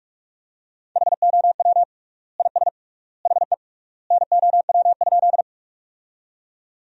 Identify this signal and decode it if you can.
Morse code, heard as a keyed tone, about 47 wpm, 700 Hz. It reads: HOW IS HE NOW?
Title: 3x Word-Spacing - Sentences from Top 100 Words - 45wpm